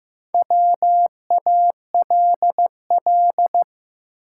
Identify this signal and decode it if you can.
Morse code, heard as a keyed tone, about 16 wpm, 700 Hz. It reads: WALL
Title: Sentences from Top 1000 Words - Review Part 6 - New Word Drill - 15wpm